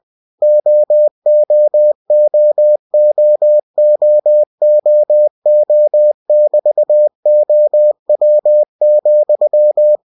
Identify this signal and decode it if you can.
Morse code, heard as a keyed tone, about 20 wpm, 600 Hz. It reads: OOOOOOO<BT>OW,
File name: cw-books / winnie-the-pooh / winnie-20wpm-page-0070